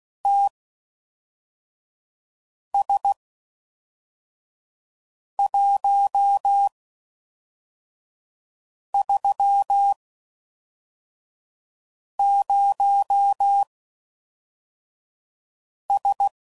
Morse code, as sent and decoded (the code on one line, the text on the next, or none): - ... .---- ...-- ----- ...
TS130S